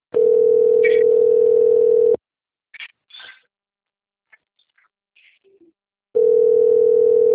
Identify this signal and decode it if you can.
Ringback tone (North America), 440 + 480 Hz.
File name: rg-1600-unknown- - - .19782